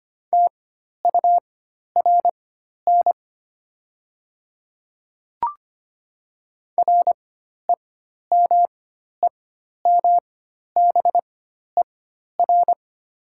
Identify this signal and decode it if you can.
Morse code, heard as a keyed tone, about 25 wpm, spaced as for 12 wpm, 700 Hz. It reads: TURN REMEMBER
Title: 4x Character Spacing - Top 500 Words - 25wpm